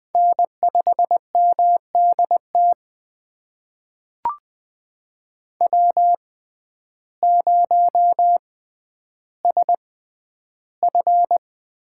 Morse code, heard as a keyed tone, about 21 wpm, spaced as for 11 wpm, 700 Hz. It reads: N5MDT W0SF
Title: USA Call Signs - 6x Character Spacing - 20wpm